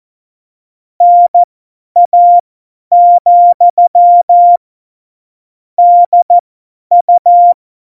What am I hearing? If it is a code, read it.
Morse code, heard as a keyed tone, about 14 wpm, spaced as for 10 wpm, 700 Hz. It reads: NA, DU
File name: Morsekurs / Witze / Witz 21 April 2020, 50 F70